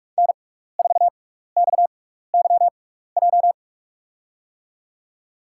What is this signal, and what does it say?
Morse code, heard as a keyed tone, about 43 wpm, spaced as for 15 wpm, 700 Hz. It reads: N4XYJ
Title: USA Call Signs - 6x Character Spacing - 45wpm